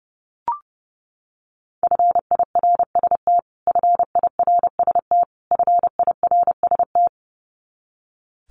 Morse code, heard as a keyed tone, about 30 wpm, 700 Hz. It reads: FIRST FIRST FIRST
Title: Top 200 Words Repeated 30wpm